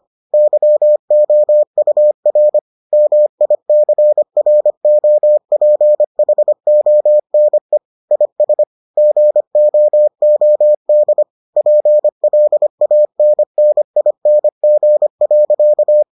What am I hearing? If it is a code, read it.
Morse code, heard as a keyed tone, about 25 wpm, 600 Hz. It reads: YOUR MICROPHONE IS GOOD PLANNING.